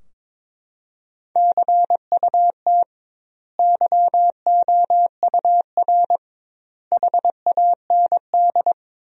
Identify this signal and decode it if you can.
Morse code, heard as a keyed tone, about 22 wpm, 700 Hz. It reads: CUT YOUR HAND
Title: 2x Word-Spacing - Sentences from Top 300 Words - 22wpm